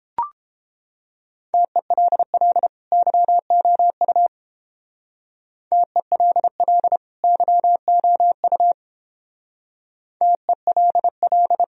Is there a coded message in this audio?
Yes: Morse code from 1.5 to 11.6 s, heard as a keyed tone, about 29 wpm, 700 Hz. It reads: TELL YOU TELL YOU TELL